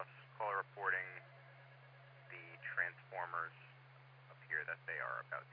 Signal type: Emergency Alert System (SAME) tones